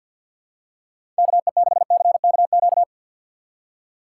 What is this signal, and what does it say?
Morse code, heard as a keyed tone, about 53 wpm, 700 Hz. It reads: KE6CKQ
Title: USA Call Signs - 6 Characters - 4x Character Spacing - 50wpm